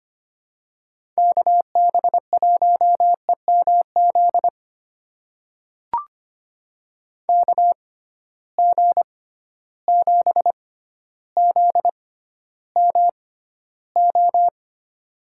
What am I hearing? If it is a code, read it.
Morse code, heard as a keyed tone, about 25 wpm, spaced as for 13 wpm, 700 Hz. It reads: KB1EMZ KG7ZMO
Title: USA Call Signs - 6 Characters - 6x Character Spacing - 25wpm